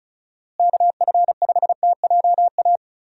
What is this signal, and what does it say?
Morse code, heard as a keyed tone, about 36 wpm, 700 Hz. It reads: KF5TJA
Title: USA Call Signs - 6 Characters - Repeated 2x - 35wpm